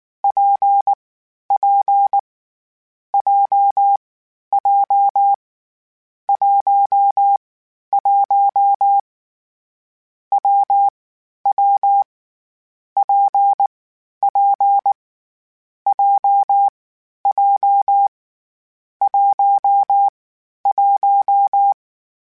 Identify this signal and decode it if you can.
Morse code, heard as a keyed tone, about 19 wpm, 800 Hz. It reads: P P J J 1 1 W W P P J J 1 1
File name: Difficult Combos / Difficult Combos-mp3s / WPJ WPJ1 @18